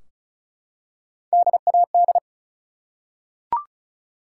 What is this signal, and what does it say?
Morse code, heard as a keyed tone, about 35 wpm, 700 Hz. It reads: DAD
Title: Three Letter Words - 35wpm